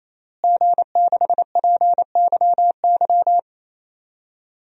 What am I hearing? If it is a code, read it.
Morse code, heard as a keyed tone, about 27 wpm, 700 Hz. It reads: G6PYY
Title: DXCC-100 Call Signs - 5 Characters - Repeated 3x - 28wpm